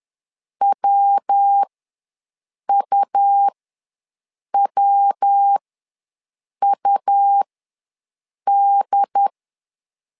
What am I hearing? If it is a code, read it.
Morse code, heard as a keyed tone, about 10 wpm, spaced as for 6 wpm, 800 Hz. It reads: WUWUD
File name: LEZIONI / 3 MWRO / 3 MW ws 10wpm 2213